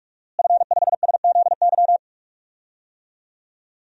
Morse code, heard as a keyed tone, about 47 wpm, 700 Hz. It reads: F5SZY
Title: DXCC-100 Call Signs - 5 Characters - Repeated 3x - 45wpm